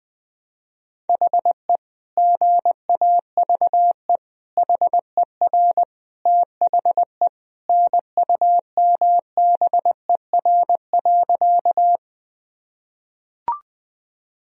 Morse code, heard as a keyed tone, about 21 wpm, 700 Hz. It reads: HE GAVE HER THE NUMBER.
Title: Sentences from Top 1000 Words - Review Part 1 - 20wpm